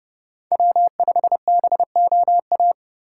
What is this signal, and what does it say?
Morse code, heard as a keyed tone, about 28 wpm, 700 Hz. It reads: W5BOA